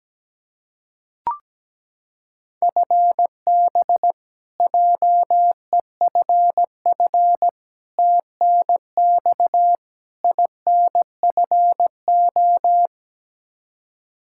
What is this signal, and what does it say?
Morse code, heard as a keyed tone, about 17 wpm, 700 Hz. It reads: FB JEFF TNX INFO
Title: Practice QSO - Exchange 3 - More Info - 17wpm